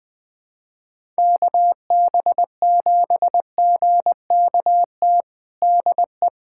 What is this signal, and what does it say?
Morse code, heard as a keyed tone, about 20 wpm, 700 Hz. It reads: KB7GKT DE
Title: USA Callsign DE Callsign 20wpm